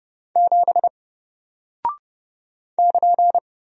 Morse code, heard as a keyed tone, about 30 wpm, 700 Hz. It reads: 7 <KN>
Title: Single Character (Letter:Number:Punctuation:Prosign) Alt 30wpm